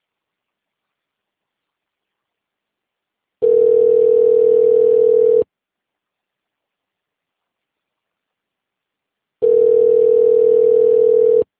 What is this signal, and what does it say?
Ringback tone (North America), 440 + 480 Hz.